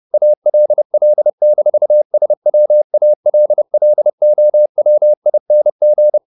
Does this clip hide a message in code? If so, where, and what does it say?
Yes: Morse code from 0.1 to 6.2 s, heard as a keyed tone, about 30 wpm, 600 Hz. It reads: ALL-SWALLOWING